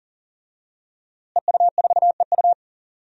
Morse code, heard as a keyed tone, about 40 wpm, 700 Hz. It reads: EU4EU
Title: DXCC-100 Call Signs - 5 Characters - Repeated 2x - 40wpm